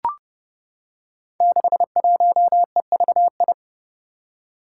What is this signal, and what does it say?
Morse code, heard as a keyed tone, about 29 wpm, 700 Hz. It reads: 61EVI
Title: Five Challenging Letter-Number Combinations 30wpm